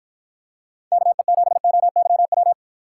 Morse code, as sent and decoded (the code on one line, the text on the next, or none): -.- . --... --- -.-- .--
KE7OYW